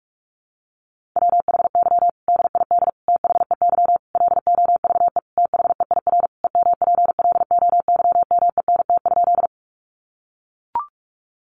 Morse code, heard as a keyed tone, about 44 wpm, 700 Hz. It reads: WHY DID THEY LOVE THEIR EMPLOYMENT?
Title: Sentences from Top 1500 Words 45wpm